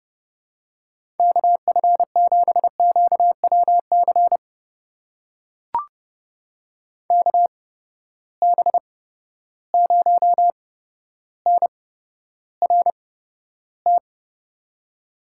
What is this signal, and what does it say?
Morse code, heard as a keyed tone, about 30 wpm, spaced as for 13 wpm, 700 Hz. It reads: KF7QWC KB0NRT
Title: USA Call Signs - 6 Characters - 8x Character Spacing - 30wpm